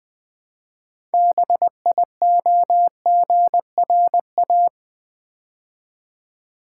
Morse code, heard as a keyed tone, about 20 wpm, 700 Hz. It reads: BIOGRA
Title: Build A Word - 3 Letters At A Time - 9-Letter Words - 20wpm